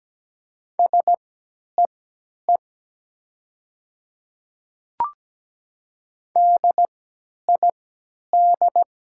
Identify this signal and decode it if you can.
Morse code, heard as a keyed tone, about 17 wpm, spaced as for 10 wpm, 700 Hz. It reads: SEE DID
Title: 3x Character Spacing - Top 300 Words - 17wpm